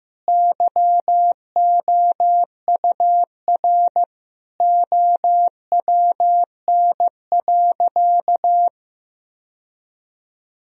Morse code, heard as a keyed tone, about 15 wpm, 700 Hz. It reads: YOUR OWN.
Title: Sentences from Top 1000 Words - Review Part 2 - 15wpm